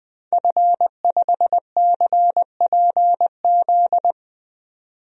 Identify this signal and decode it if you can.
Morse code, heard as a keyed tone, about 20 wpm, 700 Hz. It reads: F5CPZ
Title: DXCC-100 Call Signs - 5 Characters - 20wpm